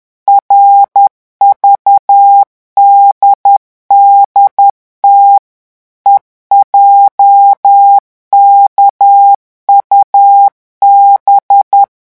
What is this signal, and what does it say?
Morse code, heard as a keyed tone, about 10 wpm, 800 Hz. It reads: RVDDT EJKUB
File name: morse mp3 / revision11 lettres vitesse 10